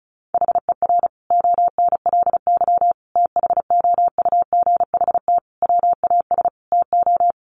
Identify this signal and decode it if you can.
Morse code, heard as a keyed tone, about 34 wpm, 700 Hz. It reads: HER ONLY THOUGHT WAS TO
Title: Sentences from Top 400 Words 35wpm